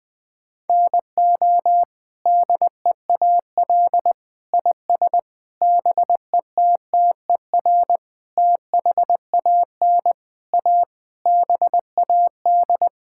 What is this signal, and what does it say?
Morse code, heard as a keyed tone, about 21 wpm, 700 Hz. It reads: NO DEAL IS BETTER THAN A BAD
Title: Sets of 8 Words - 20wpm